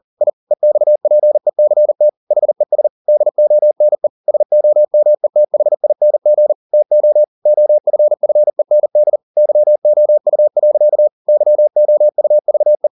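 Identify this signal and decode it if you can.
Morse code, heard as a keyed tone, about 40 wpm, 600 Hz. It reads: I EXPECT HES DONE SOMETHING TO OFFEND YOU. YOUVE
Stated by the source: Morse code at 40 wpm, practice text, CW med Gnister.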